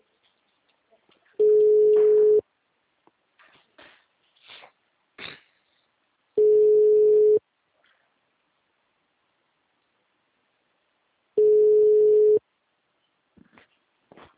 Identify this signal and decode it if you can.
Ringback tone (Europe, 425 Hz).